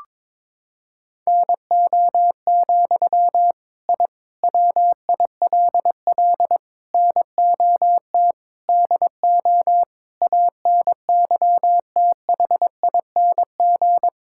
Morse code, heard as a keyed tone, about 22 wpm, 700 Hz. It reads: NO, I WILL NOT DO ANYTHING
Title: Sentences from Top 1000 Words - Review Part 6 - 22wpm